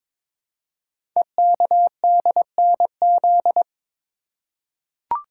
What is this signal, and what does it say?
Morse code, heard as a keyed tone, about 22 wpm, 700 Hz. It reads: EKDNZ
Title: Code Groups - 5 Letters - 22wpm